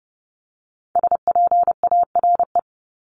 Morse code, heard as a keyed tone, about 30 wpm, 700 Hz. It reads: SPARE
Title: Top 1751-2000 Words 30wpm